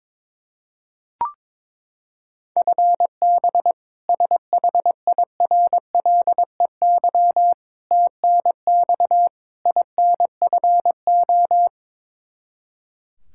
Morse code, heard as a keyed tone, about 22 wpm, 700 Hz. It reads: FB SHIRLEY TNX INFO
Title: K1USN SST QSO Practice 22wpm